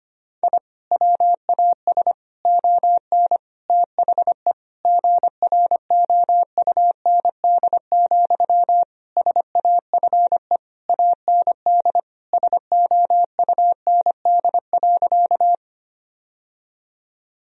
Morse code, heard as a keyed tone, about 25 wpm, 700 Hz. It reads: I WAS ON THE GROUND, SAFE AND SOUND.
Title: Sentences from Top 1000 Words - Review Part 2 - 25wpm